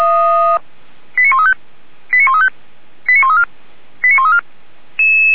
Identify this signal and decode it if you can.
Five-tone selective call (ZVEI), heard as quick runs of five tones, tones of about 70 ms. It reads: 89135 89135 89134 89134